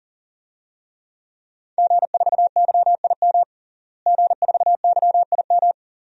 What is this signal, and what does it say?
Morse code, heard as a keyed tone, about 40 wpm, 700 Hz. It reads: G4YIM G4YIM
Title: DXCC-100 Call Signs - 5 Characters - Repeated 3x - 40wpm